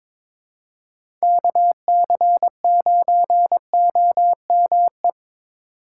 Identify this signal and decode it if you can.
Morse code, heard as a keyed tone, about 21 wpm, 700 Hz. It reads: KC9OME
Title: USA Call Signs - 6 Characters - 6x Character Spacing - 22wpm